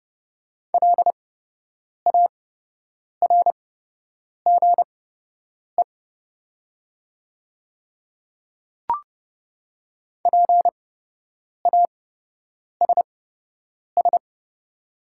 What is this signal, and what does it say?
Morse code, heard as a keyed tone, about 30 wpm, spaced as for 8 wpm, 690 Hz. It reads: LARGE PASS
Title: 8x Character Spacing - Top 500 Words - 30wpm